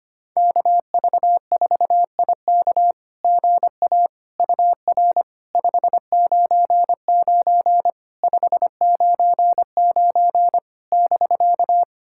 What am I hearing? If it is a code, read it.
Morse code, heard as a keyed tone, about 25 wpm, 700 Hz. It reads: KV4IK GA UR 599 599 <BK>
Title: SOTA QSO Practice 25wpm